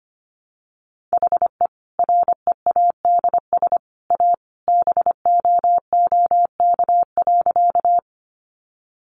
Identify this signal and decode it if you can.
Morse code, heard as a keyed tone, about 25 wpm, 700 Hz. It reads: HE READS A BOOK.
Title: Additional Easy Phrases 25wpm